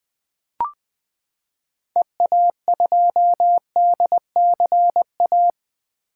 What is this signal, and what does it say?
Morse code, heard as a keyed tone, about 20 wpm, 700 Hz. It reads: EA2DCA
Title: 2023 SOTA Activator Call Signs - 20wpm